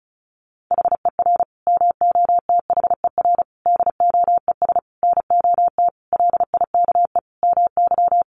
Morse code, heard as a keyed tone, about 34 wpm, 700 Hz. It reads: HER MOTHER DOES NOT LIKE MY
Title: Sentences from Top 1400 Words 35wpm